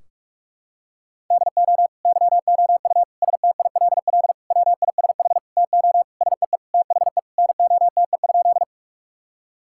Morse code, heard as a keyed tone, about 46 wpm, 700 Hz. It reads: DO YOU STILL WISH TO SEE THE NOTE?
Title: Sentences from Top 1600 Words 45wpm